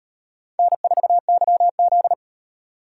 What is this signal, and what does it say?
Morse code, heard as a keyed tone, about 35 wpm, 700 Hz. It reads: N4YZ